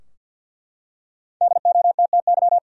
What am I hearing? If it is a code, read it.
Morse code, heard as a keyed tone, about 49 wpm, 700 Hz. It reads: DOTTY
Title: USA Common Names - Max 5 Letters - 50wpm